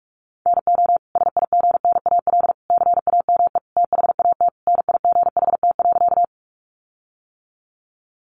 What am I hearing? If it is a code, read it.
Morse code, heard as a keyed tone, about 44 wpm, 700 Hz. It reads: NO SIGNAL CAME THAT NIGHT.
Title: Sentences from Top 1400 Words 45wpm